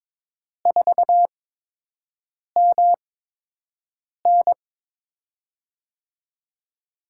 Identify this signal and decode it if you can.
Morse code, heard as a keyed tone, about 23 wpm, spaced as for 6 wpm, 700 Hz. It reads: 4MN
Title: USA Call Signs - 5 Characters - 8x Character Spacing - 22wpm